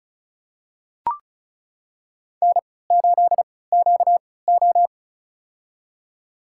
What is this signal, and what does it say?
Morse code, heard as a keyed tone, about 36 wpm, spaced as for 20 wpm, 700 Hz. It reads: N8QO
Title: USA Call Signs - 4 Characters - 3x Character Spacing - 35wpm